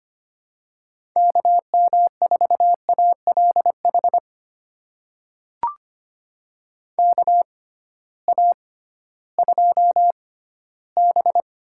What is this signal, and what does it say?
Morse code, heard as a keyed tone, about 25 wpm, spaced as for 14 wpm, 700 Hz. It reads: KM4ALH KA2B
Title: USA Call Signs - 6 Characters - 6x Character Spacing - 25wpm